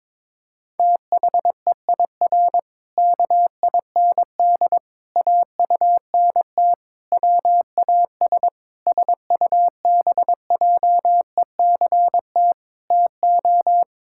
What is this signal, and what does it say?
Morse code, heard as a keyed tone, about 22 wpm, 700 Hz. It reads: THEIR KIND AUNT WAS SUBJECT TO
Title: Pangram Challenge v2 - 22wpm